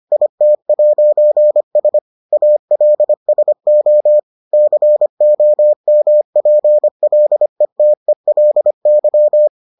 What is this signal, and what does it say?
Morse code, heard as a keyed tone, about 25 wpm, 600 Hz. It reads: IT'S ALSO COMPLETELY